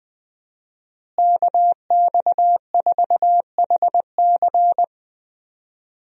Morse code, heard as a keyed tone, about 21 wpm, 700 Hz. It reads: KX4HC
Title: USA Call Signs - 5 Characters - Repeated 2x - 20wpm